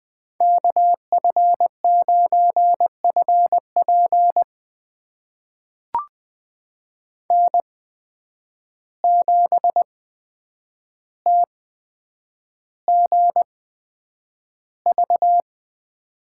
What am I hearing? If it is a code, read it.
Morse code, heard as a keyed tone, about 20 wpm, spaced as for 9 wpm, 700 Hz. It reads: KF9FP N7TGV
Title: USA Call Signs - 5 Characters - 8x Character Spacing - 20wpm